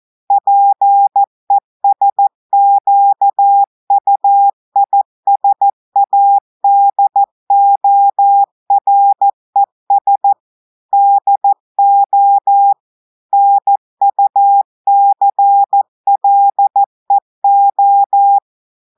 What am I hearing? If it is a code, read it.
Morse code, heard as a keyed tone, about 14 wpm, 800 Hz. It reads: PESQUISADORES DO NUCLEO
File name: audios / portugais / portugais texte en clair 1 vitesse 14 mots minutes